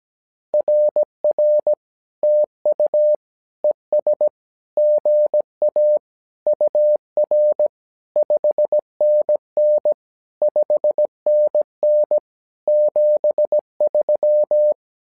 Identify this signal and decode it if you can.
Morse code, heard as a keyed tone, about 17 wpm, 600 Hz. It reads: RR TU ES GA UR 5NN 5NN 73
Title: SOTA QSO Practice 17wpm